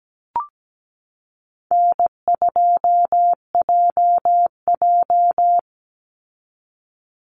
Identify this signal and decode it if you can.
Morse code, heard as a keyed tone, about 17 wpm, 700 Hz. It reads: N2JJ
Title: Easy Call Signs Repeated 2x - 17wpm